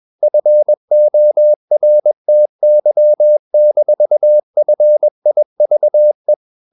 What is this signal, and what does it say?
Morse code, heard as a keyed tone, about 21 wpm, 600 Hz. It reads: FORTY-FIVE
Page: CW med Gnister